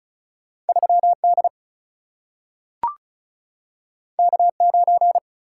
Morse code, heard as a keyed tone, about 36 wpm, 700 Hz. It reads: 3 D K 9
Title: Single-Letter Single-Number 35wpm